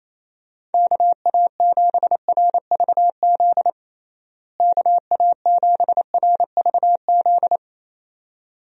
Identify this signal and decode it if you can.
Morse code, heard as a keyed tone, about 28 wpm, 700 Hz. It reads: KA7RVZ KA7RVZ
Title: USA Call Signs - 6 Characters - Repeated 2x - 28wpm